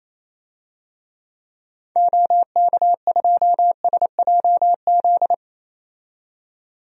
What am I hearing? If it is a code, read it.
Morse code, heard as a keyed tone, about 28 wpm, 700 Hz. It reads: OK2SJZ
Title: DXCC-100 Call Signs - 6 Characters - 28wpm